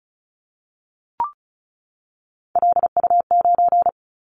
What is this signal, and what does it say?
Morse code, heard as a keyed tone, about 36 wpm, 690 Hz. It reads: LU9
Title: Callsign Prefixes for Top 50 DXCC - 35wpm